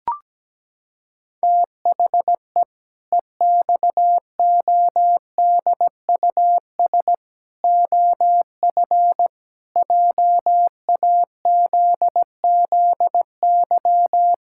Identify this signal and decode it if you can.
Morse code, heard as a keyed tone, about 17 wpm, 700 Hz. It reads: THE EXODUS OF JAZZY
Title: Pangram Challenge v2 - 17wpm